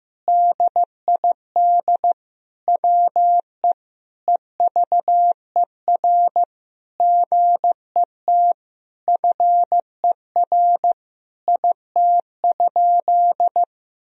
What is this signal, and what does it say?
Morse code, heard as a keyed tone, about 15 wpm, 700 Hz. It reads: DID WE EVER GET FER IT?
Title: Sentences from Top 400 Words and Abbreviations 15wpm